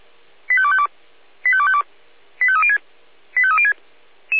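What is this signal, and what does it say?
Five-tone selective call (ZVEI), heard as quick runs of five tones, tones of about 70 ms. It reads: 85252 85252 85306 85306
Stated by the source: SDR receiver recording of ZVEI five-tone selective call 85832.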